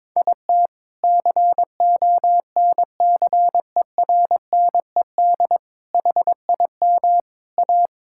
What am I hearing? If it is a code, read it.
Morse code, heard as a keyed tone, about 22 wpm, 700 Hz. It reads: IT CONCERNED HIM A
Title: Sentences from Top 1000 Words - Review Part 6 - 22wpm